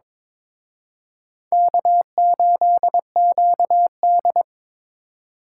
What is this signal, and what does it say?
Morse code, heard as a keyed tone, about 26 wpm, 700 Hz. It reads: K8QD
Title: Speed Racing - USA Call Signs 22wpm